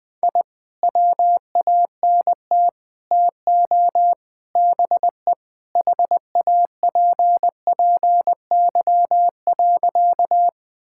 Morse code, heard as a keyed tone, about 20 wpm, 700 Hz. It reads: I WANT TO BE HAPPY.
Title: Sentences with 4-Word Phrases - 20wpm